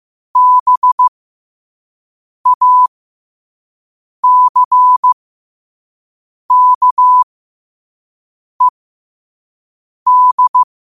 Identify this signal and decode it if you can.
Morse code, heard as a keyed tone, about 15 wpm, spaced as for 5 wpm, 1000 Hz. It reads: BACKED
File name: qotdfiles / 5wpm